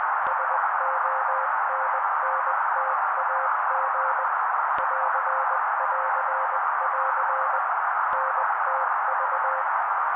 Morse code, heard as a keyed tone, about 20 wpm, 530 Hz. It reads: SONNTAG <AR><AR><AR> NTV
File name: daily / 18wpm noise